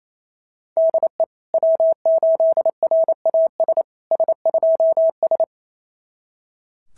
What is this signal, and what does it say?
Morse code, heard as a keyed tone, about 28 wpm, 690 Hz. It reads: DE W8RAS S2S
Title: SOTA QSO Practice 28wpm